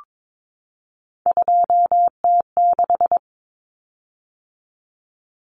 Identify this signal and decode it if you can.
Morse code, heard as a keyed tone, about 22 wpm, 700 Hz. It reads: 2T6
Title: Common Cut Numbers - 3-Digit 22wpm